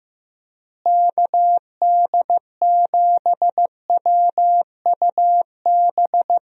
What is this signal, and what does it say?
Morse code, heard as a keyed tone, about 15 wpm, 700 Hz. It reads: KD7WUB
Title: USA Call Signs - 6 Characters - Repeated 3x - 15wpm